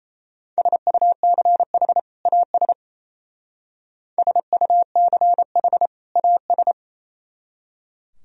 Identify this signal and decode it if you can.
Morse code, heard as a keyed tone, about 36 wpm, 700 Hz. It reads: SUCH AS SUCH AS